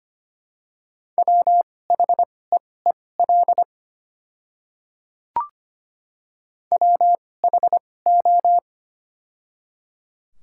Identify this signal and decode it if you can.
Morse code, heard as a keyed tone, about 25 wpm, spaced as for 18 wpm, 700 Hz. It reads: WHEEL WHO
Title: 2x Character Spacing - Top 500 Words - 25wpm